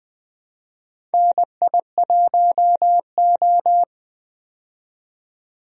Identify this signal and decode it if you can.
Morse code, heard as a keyed tone, about 20 wpm, 700 Hz. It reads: NI1O
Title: USA Call Signs - 4 Characters - 20wpm